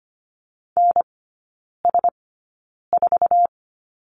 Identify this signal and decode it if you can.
Morse code, heard as a keyed tone, about 25 wpm, spaced as for 9 wpm, 700 Hz. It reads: NS4
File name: Lesson 17e - Letter u - Three Characters 25wpm